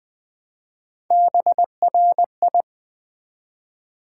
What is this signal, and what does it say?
Morse code, heard as a keyed tone, about 20 wpm, 700 Hz. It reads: BRI
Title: Build A Word - 1 Letter At A Time - 5-Letter Words - 20wpm